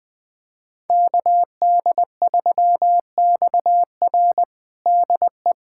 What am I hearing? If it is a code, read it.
Morse code, heard as a keyed tone, about 20 wpm, 700 Hz. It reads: KD3XR DE
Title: USA Callsign DE Callsign - 5 Characters - 20wpm